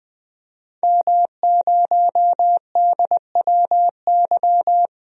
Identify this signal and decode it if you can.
Morse code, heard as a keyed tone, about 20 wpm, 700 Hz. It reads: M0DWY
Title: DXCC-100 Call Signs - 5 Characters - 20wpm